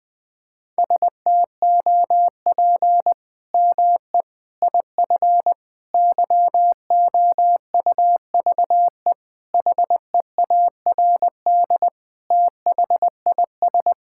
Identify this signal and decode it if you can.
Morse code, heard as a keyed tone, about 20 wpm, 700 Hz. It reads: STOP ME IF YOUVE HEARD THIS
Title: Sets of 8 Words - 20wpm